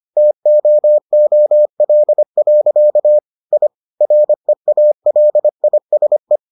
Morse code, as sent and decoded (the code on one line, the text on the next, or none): - --- --- .-.. .-.-.- / .. / .-. .- .-.. .. ... .
TOOL. I REALISE